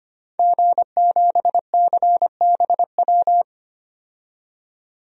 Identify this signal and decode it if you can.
Morse code, heard as a keyed tone, about 25 wpm, 700 Hz. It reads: G7CBW
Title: DXCC-100 Call Signs - 5 Characters - 25wpm